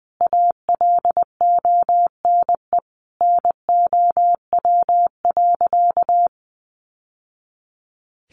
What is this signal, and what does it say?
Morse code, heard as a keyed tone, about 20 wpm, 700 Hz. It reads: ALONE NOW.
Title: Sentences from Top 1100 Words 20wpm